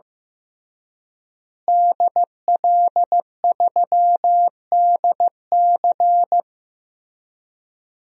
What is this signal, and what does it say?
Morse code, heard as a keyed tone, about 16 wpm, 700 Hz. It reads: DL3DC
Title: DXCC-100 Call Signs - 5 Characters - Repeated 3x - 15wpm